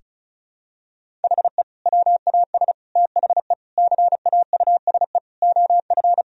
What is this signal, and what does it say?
Morse code, heard as a keyed tone, about 34 wpm, 700 Hz. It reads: HE WAS THE CAUSE OF
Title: Sentences from Top 1000 Words - Review Part 1 - 35wpm